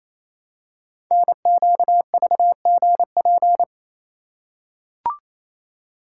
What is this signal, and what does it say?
Morse code, heard as a keyed tone, about 29 wpm, 700 Hz. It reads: NQVGP
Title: Code Groups - 5 Letters - 28wpm